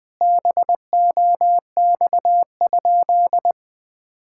Morse code, heard as a keyed tone, about 20 wpm, 700 Hz. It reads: BOX?
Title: Sentences from Top 1000 Words - Review Part 5 - 20wpm